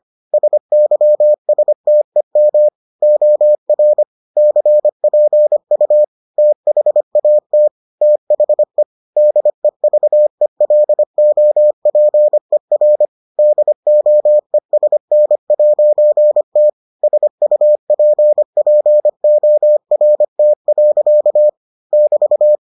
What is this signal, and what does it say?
Morse code, heard as a keyed tone, about 25 wpm, 600 Hz. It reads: SYSTEM OR CPU THAT THE DEVELOPER DOESN'T SUPPORT. <BT>